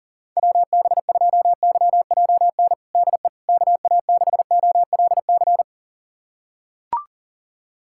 Morse code, heard as a keyed tone, about 40 wpm, 700 Hz. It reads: WB2YJN DE KA6OLC
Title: USA Callsign DE Callsign - 6 Characters - 40wpm